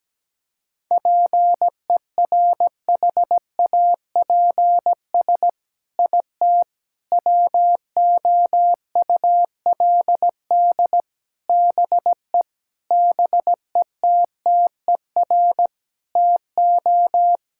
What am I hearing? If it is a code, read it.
Morse code, heard as a keyed tone, about 17 wpm, 700 Hz. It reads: PERHAPS IT WOULD BE BETTER TO
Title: Sentences from Top 1600 Words 17wpm